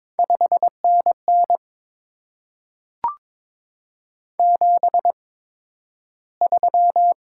Morse code, heard as a keyed tone, about 22 wpm, spaced as for 12 wpm, 700 Hz. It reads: 5NN 73
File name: Top 20 QSO Elements - 8x Character Spacing - 22wpm